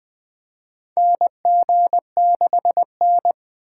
Morse code, heard as a keyed tone, about 20 wpm, 700 Hz. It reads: NG6N
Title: USA Call Signs - 4 Characters - 4x Character Spacing - 20wpm